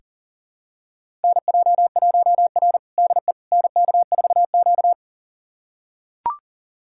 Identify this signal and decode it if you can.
Morse code, heard as a keyed tone, about 37 wpm, 700 Hz. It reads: NJ1R DE NK4Q
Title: USA Callsign DE Callsign - 4 Characters - 40wpm